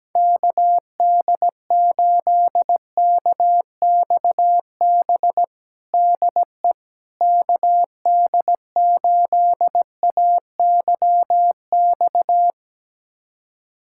Morse code, heard as a keyed tone, about 17 wpm, 700 Hz. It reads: KD8KXB DE KD8AYX
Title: USA Callsign DE Callsign 17wpm